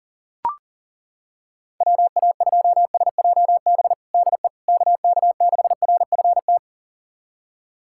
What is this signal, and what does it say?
Morse code, heard as a keyed tone, about 40 wpm, 700 Hz. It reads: WA2SJB DE KK6RFT
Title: USA Callsign DE Callsign - 6 Characters - 40wpm